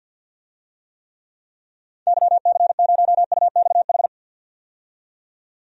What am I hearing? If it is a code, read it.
Morse code, heard as a keyed tone, about 52 wpm, 700 Hz. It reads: YC0UXH
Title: DXCC-100 Call Signs - 6 Characters - 50wpm